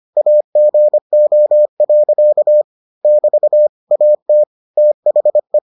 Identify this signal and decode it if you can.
Morse code, heard as a keyed tone, about 25 wpm, 600 Hz. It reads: AGO. <BT> AT THE